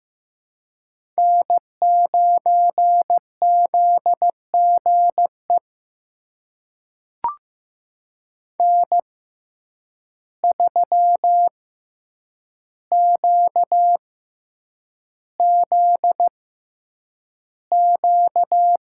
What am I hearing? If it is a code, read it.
Morse code, heard as a keyed tone, about 15 wpm, spaced as for 8 wpm, 700 Hz. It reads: N9ZGE N3QZQ
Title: USA Call Signs - 5 Characters - 6x Character Spacing - 15wpm